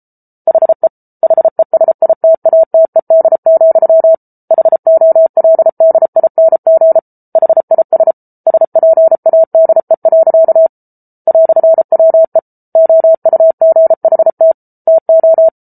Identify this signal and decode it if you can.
Morse code, heard as a keyed tone, about 34 wpm, 650 Hz. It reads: HE HESITATED, HOLDING HIS SPADE. "WE OUGHT TO